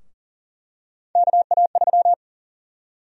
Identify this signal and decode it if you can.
Morse code, heard as a keyed tone, about 40 wpm, 700 Hz. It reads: KA3
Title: USA Call Sign Prefixes - 40wpm